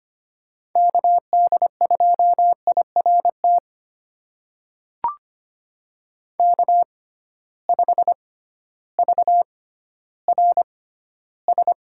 Morse code, heard as a keyed tone, about 25 wpm, spaced as for 13 wpm, 700 Hz. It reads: KD2IRT K5VRS
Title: USA Call Signs - 6x Character Spacing - 25wpm